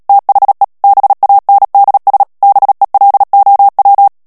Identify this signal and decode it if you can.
Morse code, heard as a keyed tone, about 37 wpm, 800 Hz. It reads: THE BANDS BELOW
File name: morse mp3 / ANGLAIS texte clair 3 vitesse 35 mots minute